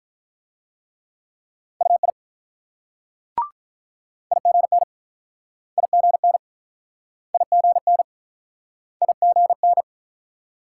Morse code, heard as a keyed tone, about 46 wpm, 700 Hz. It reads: UI IGN IGN IGN IGN